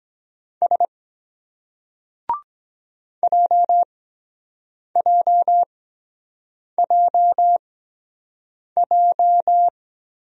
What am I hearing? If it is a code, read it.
Morse code, heard as a keyed tone, about 22 wpm, spaced as for 7 wpm, 700 Hz. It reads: S JJJJ